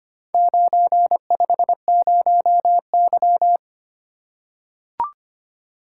Morse code, heard as a keyed tone, about 25 wpm, 690 Hz. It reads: 950Y
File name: practice / 25wpm / Four Letter-Number Combinations - 25wpm